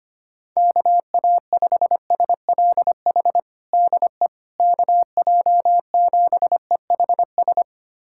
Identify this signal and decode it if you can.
Morse code, heard as a keyed tone, about 25 wpm, 700 Hz. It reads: KA5SLH DE KJ7EHS
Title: USA Callsign DE Callsign 25wpm